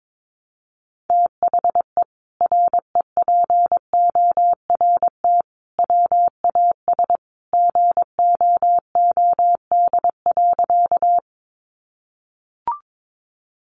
Morse code, heard as a keyed tone, about 22 wpm, 700 Hz. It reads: THE REPORT WAS GOOD.
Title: Sentences from Top 900 Words 22wpm